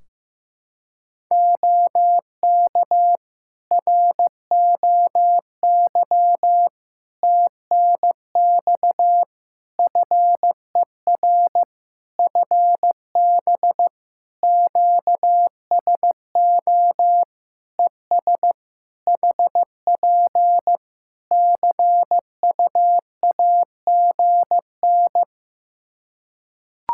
Morse code, heard as a keyed tone, about 14 wpm, 700 Hz. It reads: OK ROY TNX FER FB QSO ES HP CUAGN
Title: Practice QSO - Exchange 4 - Ending - 15wpm